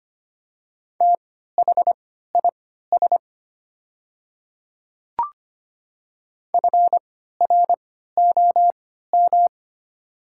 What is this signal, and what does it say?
Morse code, heard as a keyed tone, about 25 wpm, spaced as for 14 wpm, 700 Hz. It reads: THIS FROM
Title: 3x Character Spacing - Top 100 Words - 25wpm